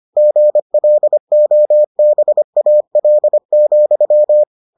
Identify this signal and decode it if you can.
Morse code, heard as a keyed tone, about 25 wpm, 600 Hz. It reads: GLOBAL,